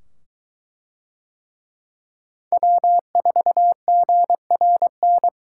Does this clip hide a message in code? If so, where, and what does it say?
Yes: Morse code from 2.5 to 5.3 s, heard as a keyed tone, about 22 wpm, 700 Hz. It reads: W4GRN